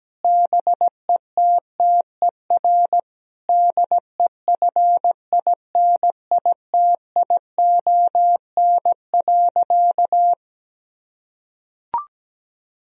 Morse code, heard as a keyed tone, about 17 wpm, 700 Hz. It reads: BETTER DEFINITION.
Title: Sentences from Top 1000 Words - Review Part 4 - 17wpm